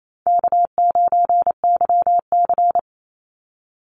Morse code, heard as a keyed tone, about 28 wpm, 700 Hz. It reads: K9YC
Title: Call Signs Repeated 3x - 28wpm